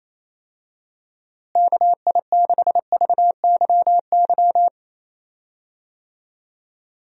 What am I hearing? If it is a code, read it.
Morse code, heard as a keyed tone, about 28 wpm, 700 Hz. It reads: KI6VYY
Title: USA Call Signs - 6 Characters - Repeated 3x - 28wpm